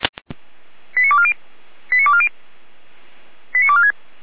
Five-tone selective call (ZVEI), heard as quick runs of five tones, tones of about 70 ms. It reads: 89240 89240 89246